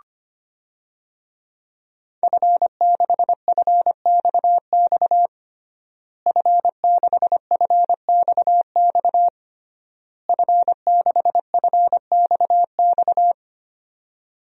Morse code, heard as a keyed tone, about 25 wpm, 700 Hz. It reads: F6FXX F6FXX F6FXX
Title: DXCC-100 Call Signs - 5 Characters - Repeated 3x - 25wpm